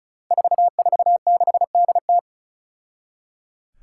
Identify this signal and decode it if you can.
Morse code, heard as a keyed tone, about 34 wpm, 700 Hz. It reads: 446DT
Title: Five Challenging Letter-Number Combinations 35wpm